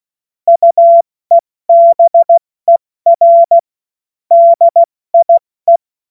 Morse code, heard as a keyed tone, about 16 wpm, spaced as for 14 wpm, 680 Hz. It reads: UEBER DIE
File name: Morsekurs / Witze / Witz 02. August 2022 70 F80